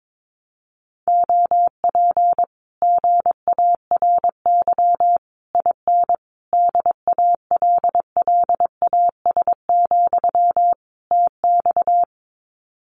Morse code, heard as a keyed tone, about 22 wpm, 700 Hz. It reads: OP GARY IN DALLAS, TX
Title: Common QSO Phrases Encore 22wpm